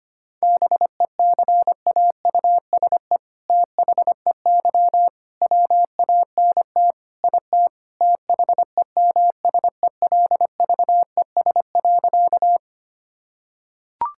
Morse code, heard as a keyed tone, about 25 wpm, 700 Hz. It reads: BECAUSE THEY WANT IT THEMSELVES.
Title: Sentences from Top 1000 Words - Review Part 5 - 25wpm